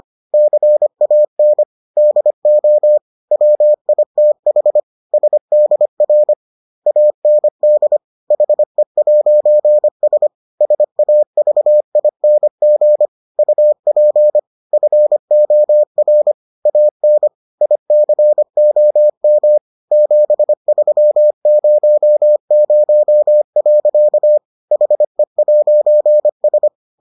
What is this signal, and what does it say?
Morse code, heard as a keyed tone, about 25 wpm, 600 Hz. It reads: CAN DO WITH SDR AND HE'S SAVING UP FOR AN ICOM 7300. HE'S